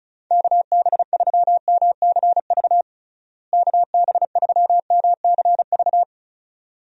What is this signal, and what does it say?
Morse code, heard as a keyed tone, about 36 wpm, 700 Hz. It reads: KB3MCV KB3MCV
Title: USA Call Signs - 6 Characters - Repeated 2x - 35wpm